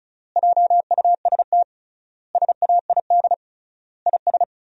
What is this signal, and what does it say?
Morse code, heard as a keyed tone, about 34 wpm, 700 Hz. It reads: JUST SAID IS
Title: 3x Word-Spacing - Sentences from Top 200 Words - 35wpm